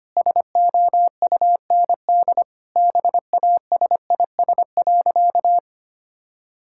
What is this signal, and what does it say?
Morse code, heard as a keyed tone, about 25 wpm, 700 Hz. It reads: SOUND BASIS.
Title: Sentences from Top 1000 Words - Review Part 5 - 25wpm